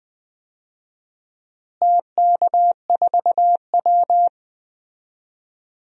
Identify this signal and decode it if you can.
Morse code, heard as a keyed tone, about 20 wpm, 700 Hz. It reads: TK4W
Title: DXCC-100 Call Signs - 4 Characters - Repeated 3x - 20wpm